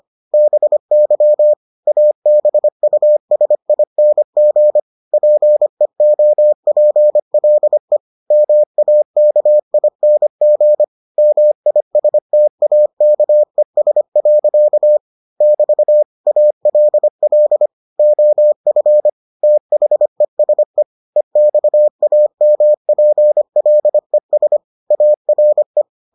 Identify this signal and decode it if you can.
Morse code, heard as a keyed tone, about 25 wpm, 600 Hz. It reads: BY ABUSING PEOPLE MAKING MISTAKES. <BT> ALL OF THESE EXAMPLES ARE